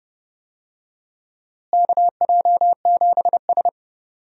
Morse code, heard as a keyed tone, about 29 wpm, 700 Hz. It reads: KJ7S
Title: USA Call Signs - 4 Characters - Repeated 2x - 30wpm